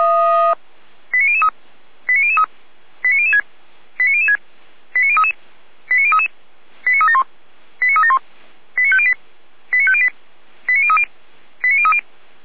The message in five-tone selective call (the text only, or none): 89003 89003 89006 89006 89033 89033 89371 89371 89508 89508 89030 89030